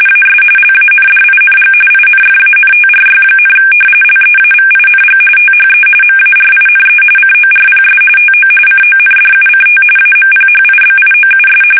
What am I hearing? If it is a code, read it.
RTTY, about 45.45 baud. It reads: AS LANDING INFANTRY, LEAVING THE MARINE CORPS TO CONDUCT AMPHIBIOUS
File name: fsk850